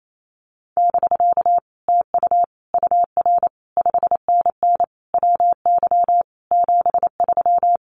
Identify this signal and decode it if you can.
Morse code, heard as a keyed tone, about 28 wpm, 700 Hz. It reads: <BK> TU UR 5NN WY 73
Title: Common QSO Phrases Encore 28wpm